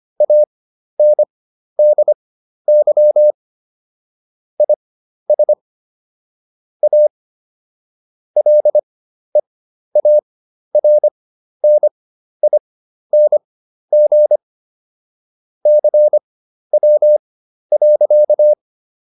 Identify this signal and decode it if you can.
Morse code, heard as a keyed tone, about 25 wpm, spaced as for 12 wpm, 600 Hz. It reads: ANDY IS A LEARNING CW.